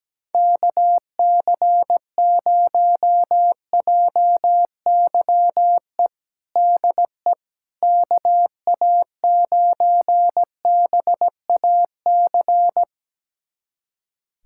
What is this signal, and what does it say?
Morse code, heard as a keyed tone, about 17 wpm, 700 Hz. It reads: KC0JYE DE KA9BAC
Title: USA Callsign DE Callsign - 6 Characters - 17wpm